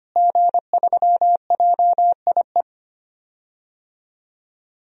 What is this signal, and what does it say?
Morse code, heard as a keyed tone, about 25 wpm, 700 Hz. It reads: G3JIE
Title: DXCC-100 Call Signs - 5 Characters - Repeated 3x - 25wpm